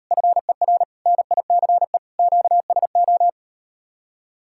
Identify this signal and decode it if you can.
Morse code, heard as a keyed tone, about 38 wpm, 700 Hz. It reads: FER NICE QSO